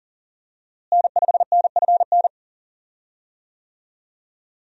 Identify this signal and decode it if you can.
Morse code, heard as a keyed tone, about 38 wpm, 700 Hz. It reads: N5NFN
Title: USA Call Signs - 5 Characters - Repeated 3x - 40wpm